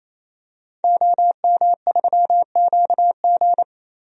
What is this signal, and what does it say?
Morse code, heard as a keyed tone, about 28 wpm, 700 Hz. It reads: OM3QG
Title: DXCC-100 Call Signs - 5 Characters - 28wpm